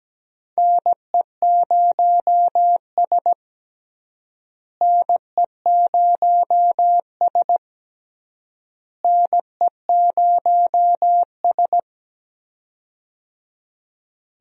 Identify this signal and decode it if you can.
Morse code, heard as a keyed tone, about 17 wpm, 700 Hz. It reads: NE0S NE0S NE0S
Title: USA Call Signs - 4 Characters - Repeated 3x - 17wpm